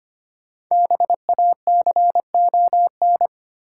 Morse code, heard as a keyed tone, about 25 wpm, 700 Hz. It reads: BACON
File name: practice / 25wpm / Difficult Words with Hints - 25wpm